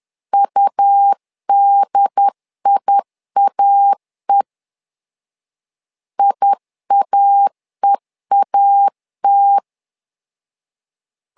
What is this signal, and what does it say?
Morse code, heard as a keyed tone, about 10 wpm, 800 Hz. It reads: UDIAE IAEAT
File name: LEZIONI / 2 UHDN / 2 UHDN 10wpm 1904